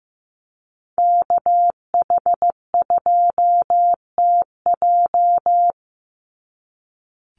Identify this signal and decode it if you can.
Morse code, heard as a keyed tone, about 15 wpm, 700 Hz. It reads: KH2TJ
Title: NA SOTA Chaser Callsigns 15wpm